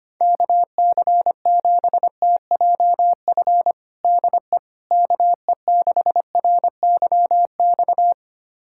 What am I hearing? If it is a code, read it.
Morse code, heard as a keyed tone, about 25 wpm, 700 Hz. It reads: KC7TJF DE KE6RYX
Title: USA Callsign DE Callsign - 6 Characters - 25wpm